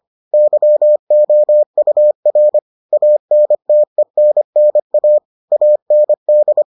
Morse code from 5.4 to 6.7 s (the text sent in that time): AND